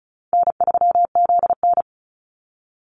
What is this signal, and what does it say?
Morse code, heard as a keyed tone, about 34 wpm, 700 Hz. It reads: N3ZN
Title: Easy Call Signs Repeated 2x - 35wpm